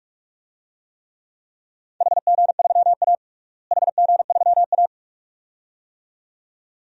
Morse code, heard as a keyed tone, about 46 wpm, 700 Hz. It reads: HG3A HG3A
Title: DXCC-100 Call Signs - 4 Characters - Repeated 2x - 45wpm